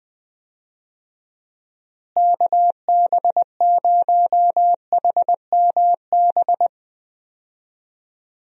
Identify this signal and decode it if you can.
Morse code, heard as a keyed tone, about 20 wpm, 700 Hz. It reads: KB0HMB
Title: USA Call Signs - 6 Characters - Repeated 2x - 20wpm